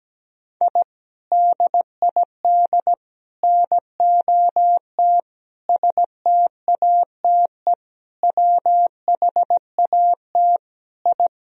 Morse code, heard as a keyed tone, about 17 wpm, 700 Hz. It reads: I DID NOT STATE WHAT I
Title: Sentences from Top 1000 Words - Review Part 2 - 17wpm